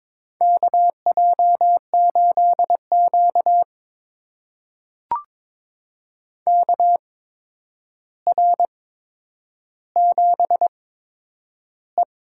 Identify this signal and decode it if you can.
Morse code, heard as a keyed tone, about 22 wpm, spaced as for 9 wpm, 700 Hz. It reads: KJ8Q KR7E
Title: USA Call Signs - 4 Characters - 8x Character Spacing - 22wpm